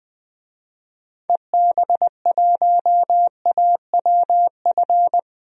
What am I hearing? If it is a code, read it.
Morse code, heard as a keyed tone, about 20 wpm, 700 Hz. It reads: EB1AWF
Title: DXCC-100 Call Signs - 6 Characters - 20wpm